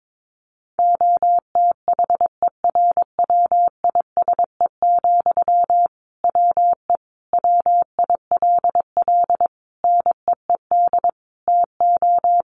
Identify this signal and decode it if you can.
Morse code, heard as a keyed tone, about 22 wpm, 700 Hz. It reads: OTHERWISE, WE WILL NEED TO
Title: Sentences from Top 1500 Words 22wpm